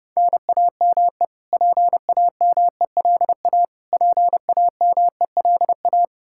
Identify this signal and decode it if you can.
Morse code, heard as a keyed tone, about 28 wpm, 700 Hz. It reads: NAME PAMELA PAMELA
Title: Practice QSO - Exchange 1 - Beginning - 30wpm